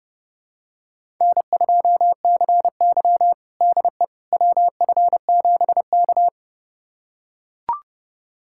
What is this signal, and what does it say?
Morse code, heard as a keyed tone, about 30 wpm, 700 Hz. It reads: N2CY DE WF7K
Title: USA Callsign DE Callsign - 4 Characters - 30wpm